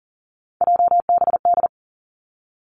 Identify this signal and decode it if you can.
Morse code, heard as a keyed tone, about 38 wpm, 700 Hz. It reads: JBD
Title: Three Challenging Letter-Number Combinations 40wpm